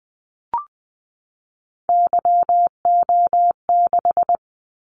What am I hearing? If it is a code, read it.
Morse code, heard as a keyed tone, about 20 wpm, 700 Hz. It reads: YO6
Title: Callsign Prefixes for Top 25 DXCC - 20wpm